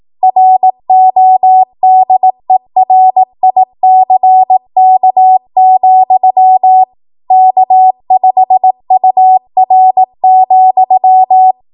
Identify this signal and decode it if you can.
Morse code, heard as a keyed tone, about 18 wpm, 750 Hz. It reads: RODERICK, K5UR,
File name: Archive / 18 WPM / 18WPM